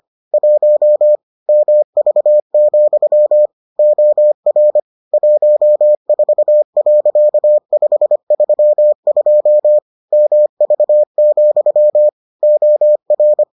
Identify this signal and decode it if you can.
Morse code, heard as a keyed tone, about 25 wpm, 600 Hz. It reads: 1 MV, OR 14.532 MV, OR